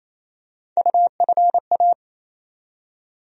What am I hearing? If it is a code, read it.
Morse code, heard as a keyed tone, about 28 wpm, 700 Hz. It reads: UFA
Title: USA Call Sign Postfixes - 3 Letters 28wpm